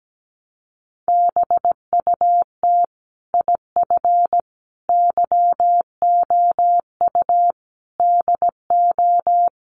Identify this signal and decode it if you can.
Morse code, heard as a keyed tone, about 17 wpm, 700 Hz. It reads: BUT IF YOU DO
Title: Sets of 5 Words Encore 17wpm